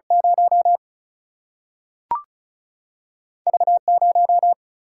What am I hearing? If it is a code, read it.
Morse code, heard as a keyed tone, about 36 wpm, 700 Hz. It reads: 0 V 0
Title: Single-Letter Single-Number 35wpm